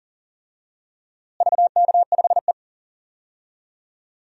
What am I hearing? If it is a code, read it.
Morse code, heard as a keyed tone, about 40 wpm, 700 Hz. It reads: VK5E
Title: DXCC-100 Call Signs - 4 Characters - Repeated 2x - 40wpm